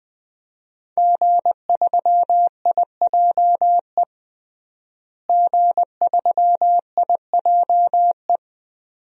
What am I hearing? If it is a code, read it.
Morse code, heard as a keyed tone, about 20 wpm, 700 Hz. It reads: G3IJE G3IJE
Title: DXCC-100 Call Signs - 5 Characters - Repeated 2x - 20wpm